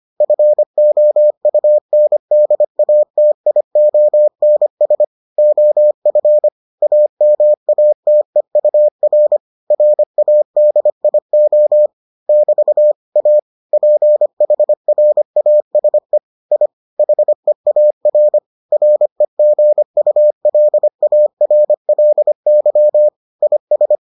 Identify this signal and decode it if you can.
Morse code, heard as a keyed tone, about 25 wpm, 600 Hz. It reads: FOUNDATIONS OF AMATEUR RADIO <BT> A PHRASE I HEAR REGULARLY IS